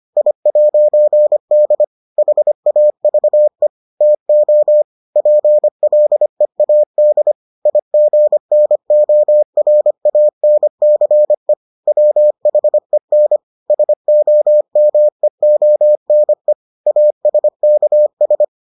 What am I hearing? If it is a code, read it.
Morse code, heard as a keyed tone, about 25 wpm, 600 Hz. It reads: I'D HAVE TO PLEAD IGNORANCE WHEN SOMEONE ASKS